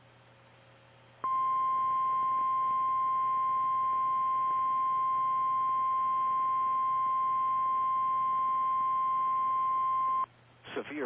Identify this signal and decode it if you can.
Emergency Alert System (SAME) tones (the 1050 Hz attention tone), with speech. It is heard from 1.2 to 10.3 s.
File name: eas / KKPRKXPN / KKPRKXPN